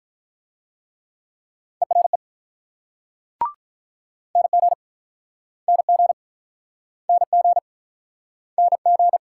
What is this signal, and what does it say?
Morse code, heard as a keyed tone, about 46 wpm, 690 Hz. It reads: ERE NG NG NG NG